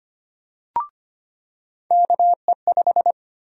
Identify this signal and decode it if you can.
Morse code, heard as a keyed tone, about 25 wpm, 700 Hz. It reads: KE5
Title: USA Call Sign Prefixes - 25wpm